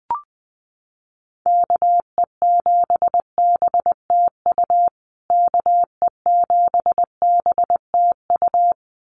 Morse code, heard as a keyed tone, about 20 wpm, 700 Hz. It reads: KE7BTU KE7BTU
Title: USA Call Signs Repeated 2x 20wpm